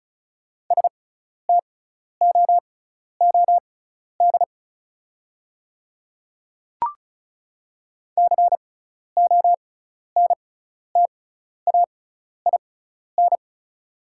Morse code, heard as a keyed tone, about 36 wpm, spaced as for 12 wpm, 700 Hz. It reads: STOOD CONTAIN
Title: 6x Character Spacing - Top 500 Words - 35wpm